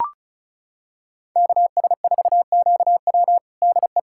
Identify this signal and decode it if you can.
Morse code, heard as a keyed tone, about 36 wpm, 700 Hz. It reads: KS4QW DE
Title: USA Callsign DE Callsign - 5 Characters - 35wpm